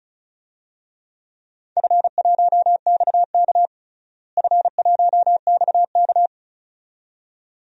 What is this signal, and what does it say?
Morse code, heard as a keyed tone, about 34 wpm, 700 Hz. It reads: F1XK F1XK
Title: DXCC-100 Call Signs - 4 Characters - Repeated 2x - 35wpm